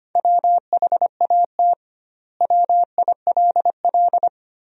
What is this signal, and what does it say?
Morse code, heard as a keyed tone, about 25 wpm, 700 Hz. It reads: WHAT WILL
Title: 2x Word-Spacing - Sentences from Top 200 Words - 25wpm